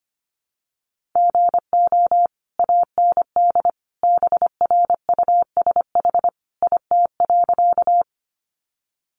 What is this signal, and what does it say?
Morse code, heard as a keyed tone, about 25 wpm, 700 Hz. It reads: GO AND BRUSH IT.
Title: Sentences from Top 1100 Words 25wpm